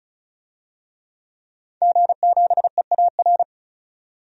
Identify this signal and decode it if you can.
Morse code, heard as a keyed tone, about 35 wpm, 700 Hz. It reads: G7EAR
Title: DXCC-100 Call Signs - 5 Characters - 35wpm